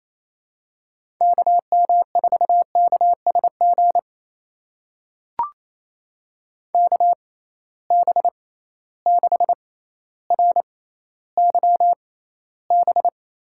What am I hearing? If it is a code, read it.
Morse code, heard as a keyed tone, about 28 wpm, spaced as for 14 wpm, 700 Hz. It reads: KM4KSG KB6RYB
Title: USA Call Signs - 6 Characters - 6x Character Spacing - 28wpm